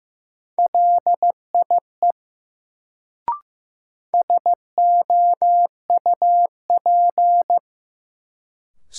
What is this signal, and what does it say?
Morse code, heard as a keyed tone, about 15 wpm, 700 Hz. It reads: LIE SOUP
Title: Sentences from Top 1600 Words - New Word Drill Alt - 15wpm